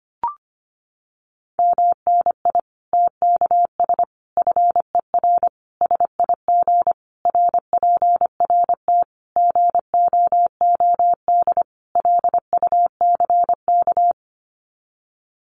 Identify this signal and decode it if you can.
Morse code, heard as a keyed tone, about 25 wpm, 700 Hz. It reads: MNI TKS FER SIG RPRT GOOD LUCK
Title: Common QSO Phrases 25wpm